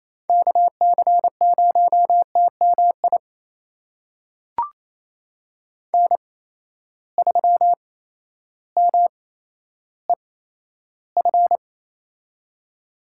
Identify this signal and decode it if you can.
Morse code, heard as a keyed tone, about 28 wpm, spaced as for 13 wpm, 700 Hz. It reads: KC0TMI N3MEF
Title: USA Call Signs - 8x Character Spacing - 28wpm